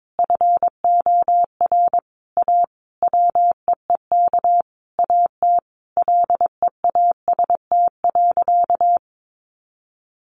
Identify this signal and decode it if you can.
Morse code, heard as a keyed tone, about 22 wpm, 700 Hz. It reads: FOR A WEEK AT LEAST.
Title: Sentences from Top 900 Words 22wpm